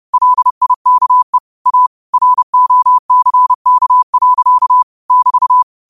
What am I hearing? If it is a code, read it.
Morse code, heard as a keyed tone, about 30 wpm, 1000 Hz. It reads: LIKE A ROCK. <BT>
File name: qotdfiles / 30wpm